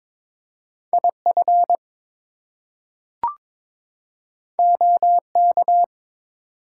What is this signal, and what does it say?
Morse code, heard as a keyed tone, about 22 wpm, 700 Hz. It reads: IF OK
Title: Two Letter Words - 22wpm